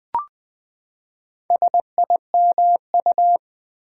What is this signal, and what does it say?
Morse code, heard as a keyed tone, about 20 wpm, 700 Hz. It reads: SIMU
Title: Build A Word - 4 Letters At A Time - 10-Letter Words - 20wpm